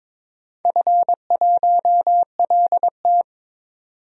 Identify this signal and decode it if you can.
Morse code, heard as a keyed tone, about 22 wpm, 700 Hz. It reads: F1LT
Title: DXCC-100 Call Signs - 4 Characters - 22wpm